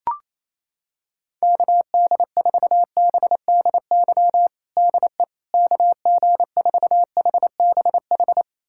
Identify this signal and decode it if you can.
Morse code, heard as a keyed tone, about 28 wpm, 700 Hz. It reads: KD4BDY DE KG4HBH
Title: USA Callsign DE Callsign 28wpm